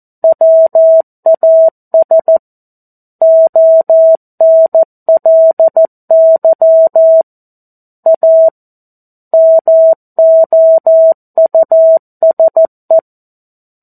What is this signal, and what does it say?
Morse code, heard as a keyed tone, about 14 wpm, 650 Hz. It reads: WAS ONLY A MOUSE
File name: 02-Chapter-02 14 WPM